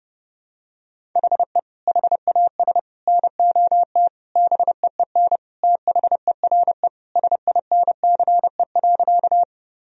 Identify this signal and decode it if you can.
Morse code, heard as a keyed tone, about 32 wpm, 700 Hz. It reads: HE HAS NOT BEEN THERE SINCE.
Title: Sentences from Top 1000 Words - Review Part 1 - 30wpm